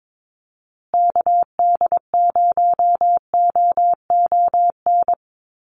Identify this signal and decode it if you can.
Morse code, heard as a keyed tone, about 22 wpm, 700 Hz. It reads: KD0OON
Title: USA Call Signs Repeated 2x 22wpm